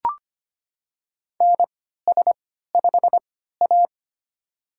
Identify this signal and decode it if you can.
Morse code, heard as a keyed tone, about 25 wpm, spaced as for 14 wpm, 700 Hz. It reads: NS5A
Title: USA Call Signs - 4 Characters - 3x Character Spacing - 25wpm